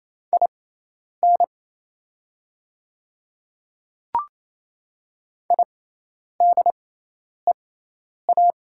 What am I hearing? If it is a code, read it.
Morse code, heard as a keyed tone, about 27 wpm, spaced as for 10 wpm, 700 Hz. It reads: IN IDEA
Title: 6x Character Spacing - Top 300 Words - 28wpm